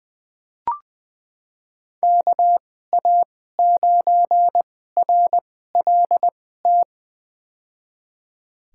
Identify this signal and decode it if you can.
Morse code, heard as a keyed tone, about 21 wpm, spaced as for 15 wpm, 700 Hz. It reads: KA9RLT
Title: USA Call Signs - 6 Characters - 2x Character Spacing - 20wpm